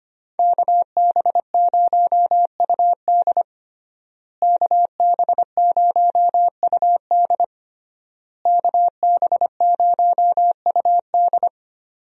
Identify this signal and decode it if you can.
Morse code, heard as a keyed tone, about 25 wpm, 700 Hz. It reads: KB0UD KB0UD KB0UD
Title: USA Call Signs - 5 Characters - Repeated 3x - 25wpm